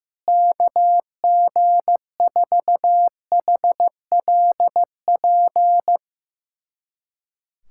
Morse code, heard as a keyed tone, about 15 wpm, 700 Hz. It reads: KG4HLP
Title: USA Callsign DE Callsign - 6 Characters - 15wpm